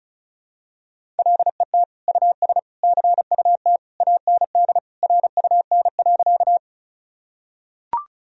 Morse code, heard as a keyed tone, about 36 wpm, 700 Hz. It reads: LET US CUT AND RUN.
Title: Sentences from Top 1000 Words - Review Part 1 - 35wpm